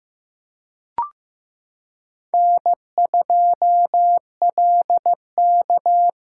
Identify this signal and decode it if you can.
Morse code, heard as a keyed tone, about 15 wpm, 700 Hz. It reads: N2LK
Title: K1USN SST QSO Practice - 15wpm